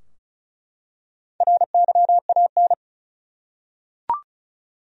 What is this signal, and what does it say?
Morse code, heard as a keyed tone, about 34 wpm, 700 Hz. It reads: RYAN
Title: USA Common Names - Max 5 Letters - 35wpm